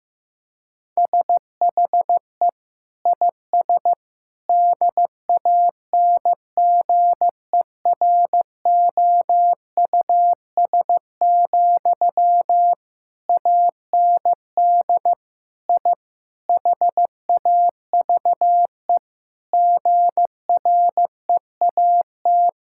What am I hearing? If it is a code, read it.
Morse code, heard as a keyed tone, about 15 wpm, 700 Hz. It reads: SHE IS DANGEROUS, AND I HAVE GREAT
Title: Sentences from Top 1600 Words 15wpm